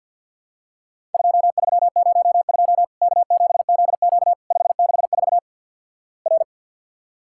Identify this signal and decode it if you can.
Morse code, heard as a keyed tone, about 51 wpm, 700 Hz. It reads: 2302 K7ZQ 564 R
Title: Worked All Europe DX Contest - QTC Exchange - 50wpm